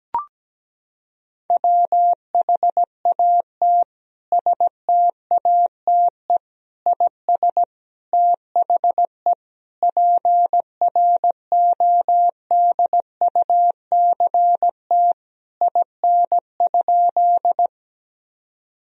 Morse code, heard as a keyed tone, about 17 wpm, 700 Hz. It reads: WHAT STATE IS THE PRODUCT IN?
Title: Sentences from Top 1000 Words - Review Part 2 - 17wpm